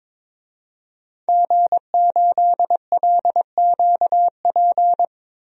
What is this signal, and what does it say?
Morse code, heard as a keyed tone, about 22 wpm, 700 Hz. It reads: G8LQP
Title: DXCC-100 Call Signs - 5 Characters - Repeated 2x - 22wpm